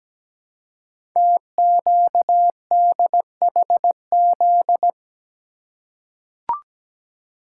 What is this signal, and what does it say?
Morse code, heard as a keyed tone, about 17 wpm, 700 Hz. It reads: TQDHZ
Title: Code Groups - 5 Letter:Numbers - 17wpm